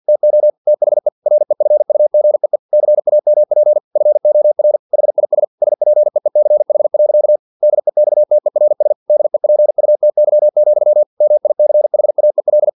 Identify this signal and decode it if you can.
Morse code, heard as a keyed tone, about 50 wpm, 600 Hz. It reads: TO THE REFUGEE CAMP FOR HIS SPEECH. DEXTERS DEPUTY, MICHAEL
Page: Never 49wpm